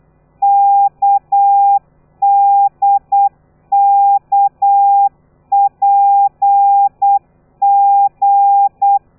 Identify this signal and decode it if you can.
Morse code, heard as a keyed tone, about 8 wpm, 800 Hz. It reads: KDKPG
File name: telegrafia / i2tfj / Lez13 35